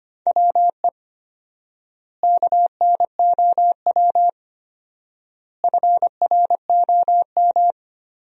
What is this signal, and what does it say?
Morse code, heard as a keyed tone, about 25 wpm, 700 Hz. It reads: WE KNOW FROM
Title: 4x Word-Spacing - Sentences from Top 500 Words - 25wpm